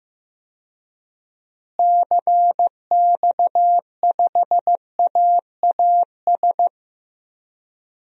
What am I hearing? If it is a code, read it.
Morse code, heard as a keyed tone, about 15 wpm, 700 Hz. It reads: CX5AAS
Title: DXCC-100 Call Signs - 6 Characters - Repeated 3x - 15wpm